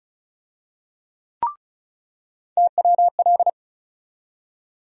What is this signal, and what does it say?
Morse code, heard as a keyed tone, about 36 wpm, 700 Hz. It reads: TWL
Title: USA Call Sign Postfixes - 3 Letters 35wpm